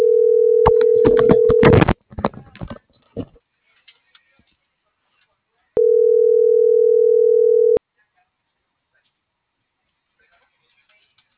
Ringback tone (North America), 440 + 480 Hz.